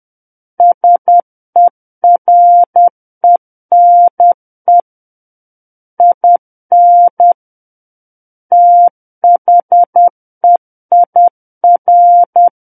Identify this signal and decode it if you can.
Morse code, heard as a keyed tone, about 10 wpm, 700 Hz. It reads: SERENE IN THEIR
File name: Book One / 01-Chapter 1 - 10 WPM